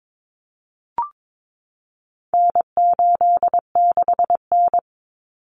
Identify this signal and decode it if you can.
Morse code, heard as a keyed tone, about 22 wpm, 700 Hz. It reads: N86N
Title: Common Cut Numbers - 4-Digit 22wpm